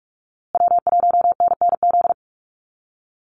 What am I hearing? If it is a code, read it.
Morse code, heard as a keyed tone, about 44 wpm, 700 Hz. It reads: W1NNZ
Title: USA Call Signs Repeated 3x 45wpm